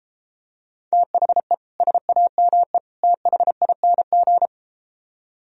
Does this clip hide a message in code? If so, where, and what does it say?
Yes: Morse code from 0.9 to 4.5 s, heard as a keyed tone, about 33 wpm, 700 Hz. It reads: THE SAME THING